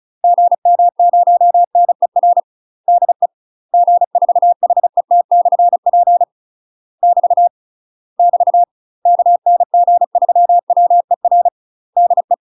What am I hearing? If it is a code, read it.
Morse code, heard as a keyed tone, about 34 wpm, 700 Hz. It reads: GM0NER DE G4HET/P <BT> <BT> KN G3WER DE
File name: UK MASTER QSOs TRAINING / 35WPM / UK MASTER QSOs 0037